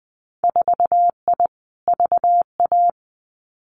Morse code, heard as a keyed tone, about 20 wpm, 700 Hz. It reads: 4I VA
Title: Winter Field Day Exchanges 20wpm